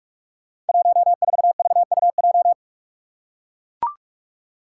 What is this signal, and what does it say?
Morse code, heard as a keyed tone, about 42 wpm, 700 Hz. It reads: 14VUJ
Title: Five Challenging Letter-Number Combinations 45wpm